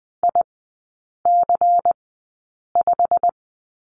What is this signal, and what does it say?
Morse code, heard as a keyed tone, about 20 wpm, spaced as for 8 wpm, 690 Hz. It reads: IC5
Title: 093 Lesson 16e - Letter c - Three Characters 20wpm